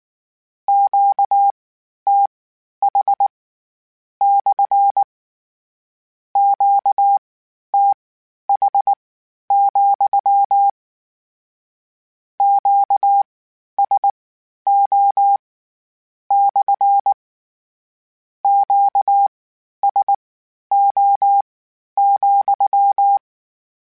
Morse code, heard as a keyed tone, about 19 wpm, spaced as for 11 wpm, 800 Hz. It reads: QTH / QTH, QSO / QSO,